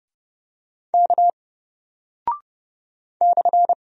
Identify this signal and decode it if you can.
Morse code, heard as a keyed tone, about 30 wpm, 700 Hz. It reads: K /
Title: Single Character (Letter:Number:Punctuation:Prosign) 30wpm